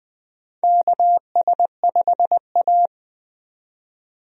Morse code, heard as a keyed tone, about 20 wpm, 700 Hz. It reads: KS5A
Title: USA Call Signs - 4 Characters - 20wpm